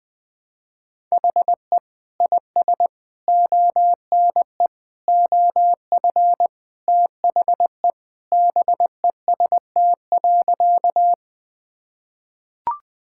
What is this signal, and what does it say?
Morse code, heard as a keyed tone, about 20 wpm, 700 Hz. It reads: HE IS ONE OF THE BEST.
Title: Sentences with 4-Word Phrases - 20wpm